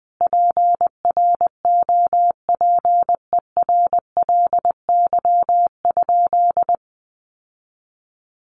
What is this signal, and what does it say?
Morse code, heard as a keyed tone, about 19 wpm, 700 Hz. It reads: PROPERLY?
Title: Sentences from Top 1100 Words 20wpm